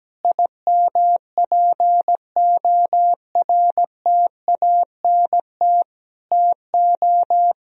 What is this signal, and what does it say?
Morse code, heard as a keyed tone, about 17 wpm, 700 Hz. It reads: IMPORTANT TO
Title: Sentences from Top 1000 Words - Review Part 4 - 17wpm